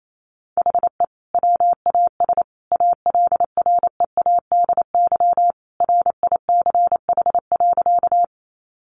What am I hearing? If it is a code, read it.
Morse code, heard as a keyed tone, about 28 wpm, 700 Hz. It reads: HE WAS ALREADY RICH.
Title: Sentences from Top 1100 Words 28wpm